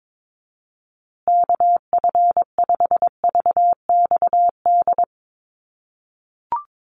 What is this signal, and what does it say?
Morse code, heard as a keyed tone, about 22 wpm, 700 Hz. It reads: KF5VXD
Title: USA Call Signs Repeated 3x 22wpm